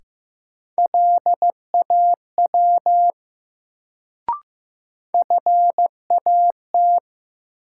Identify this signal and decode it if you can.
Morse code, heard as a keyed tone, about 15 wpm, 700 Hz. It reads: LAW FAT
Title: Sentences from Top 600 Words - New Word Drill - 15wpm